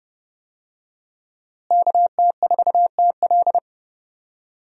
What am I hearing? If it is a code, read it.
Morse code, heard as a keyed tone, about 30 wpm, 700 Hz. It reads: KT4TL
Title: USA Call Signs - 5 Characters - Repeated 2x - 30wpm